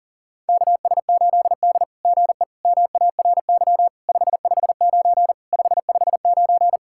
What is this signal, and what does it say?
Morse code, heard as a keyed tone, about 39 wpm, 700 Hz. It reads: KS8D GE MARY 559 559
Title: SOTA QSO Practice 40wpm